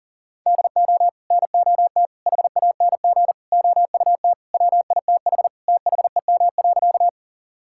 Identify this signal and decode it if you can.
Morse code, heard as a keyed tone, about 37 wpm, 700 Hz. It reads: DO NOT HANG OUT WITH THEM.
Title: Sentences from Top 1600 Words 40wpm